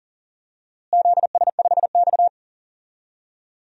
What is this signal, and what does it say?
Morse code, heard as a keyed tone, about 37 wpm, 700 Hz. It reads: ZS5X
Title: DXCC-100 Call Signs - 4 Characters - Repeated 3x - 40wpm